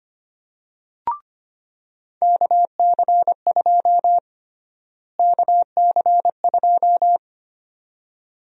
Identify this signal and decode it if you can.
Morse code, heard as a keyed tone, about 25 wpm, 700 Hz. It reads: KC2 KC2
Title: USA Call Sign Prefixes - Repeated 2x - 25wpm